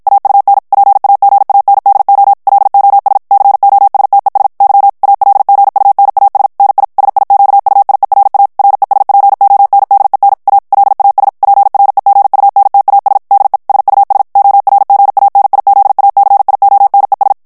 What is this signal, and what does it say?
Morse code, heard as a keyed tone, about 56 wpm, 800 Hz. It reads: AUN PAGANDO LOS COSTES Y ALGUNAS NI SIQUIERA RESPONDEN A LAS PREGUNTAS DE SUS ORGANIZACIONES
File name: ESPAGNOL TEXTE EN CLAIR 1 VITESSE 50 MOTS MINUTES